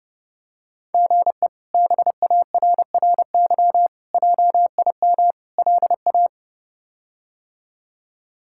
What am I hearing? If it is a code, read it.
Morse code, heard as a keyed tone, about 30 wpm, 700 Hz. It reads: GE BARRY JIM LA
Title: K1USN SST QSO Practice - 30wpm